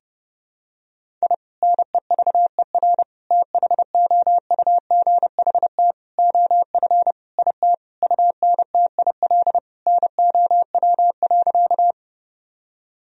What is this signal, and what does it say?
Morse code, heard as a keyed tone, about 30 wpm, 700 Hz. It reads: I NEVER THOUGHT OF IT UNTIL NOW.
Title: Sentences from Top 1000 Words - Review Part 1 - 30wpm